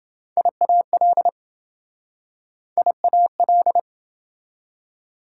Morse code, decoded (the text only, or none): IAL IAL